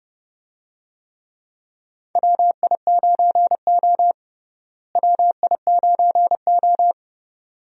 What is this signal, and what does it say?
Morse code, heard as a keyed tone, about 30 wpm, 700 Hz. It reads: WI9O WI9O
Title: USA Call Signs - 4 Characters - Repeated 2x - 30wpm